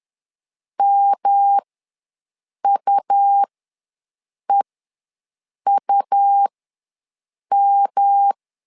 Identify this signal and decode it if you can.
Morse code, heard as a keyed tone, about 10 wpm, spaced as for 6 wpm, 800 Hz. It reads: MUEUM
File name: LEZIONI / 3 MWRO / 3 M ws 10wpm 2121